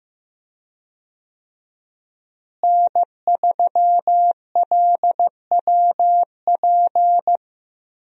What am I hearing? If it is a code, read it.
Morse code, heard as a keyed tone, about 15 wpm, 700 Hz. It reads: N3LWP
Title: USA Call Signs - 5 Characters - Repeated 2x - 15wpm